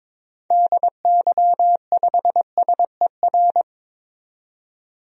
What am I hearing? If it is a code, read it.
Morse code, heard as a keyed tone, about 22 wpm, 700 Hz. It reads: DY5SER
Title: DXCC-100 Call Signs - 6 Characters - 22wpm